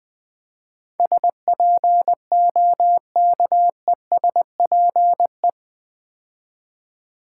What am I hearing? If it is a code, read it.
Morse code, heard as a keyed tone, about 20 wpm, 700 Hz. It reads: SPOKESPE
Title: Build A Word - 4 Letters At A Time - 12-Letter Words - 20wpm